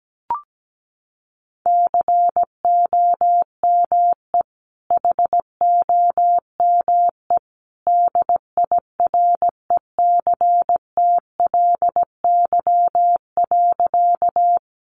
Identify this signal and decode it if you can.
Morse code, heard as a keyed tone, about 17 wpm, 700 Hz. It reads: COME HOME DIRECTLY.
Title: Sentences from Top 800 Words 17wpm